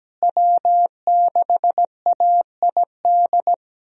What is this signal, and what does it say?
Morse code, heard as a keyed tone, about 17 wpm, 700 Hz. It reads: W6AID
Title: USA Call Signs - 5 Characters - Repeated 3x - 17wpm